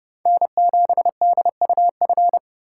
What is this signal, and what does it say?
Morse code, heard as a keyed tone, about 28 wpm, 700 Hz. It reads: N7DUF